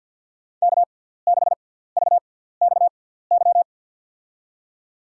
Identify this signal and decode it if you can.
Morse code, heard as a keyed tone, about 51 wpm, spaced as for 17 wpm, 690 Hz. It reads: K6VXY
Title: USA Call Signs - 5 Characters - 6x Character Spacing - 50wpm